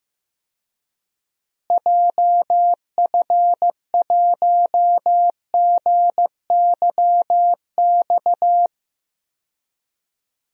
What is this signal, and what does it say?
Morse code, heard as a keyed tone, about 15 wpm, 700 Hz. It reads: JF1GYX
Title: DXCC-100 Call Signs - 6 Characters - Repeated 3x - 15wpm